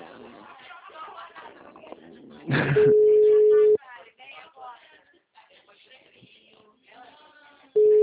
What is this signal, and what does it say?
Ringback tone (Europe, 425 Hz), with speech.